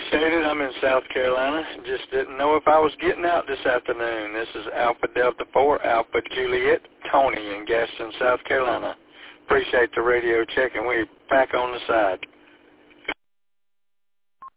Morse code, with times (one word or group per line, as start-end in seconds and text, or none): none